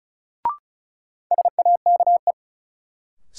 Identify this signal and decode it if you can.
Morse code, heard as a keyed tone, about 34 wpm, 700 Hz. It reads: SAKE
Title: Sets of 4 Words - New Word Drill - 35wpm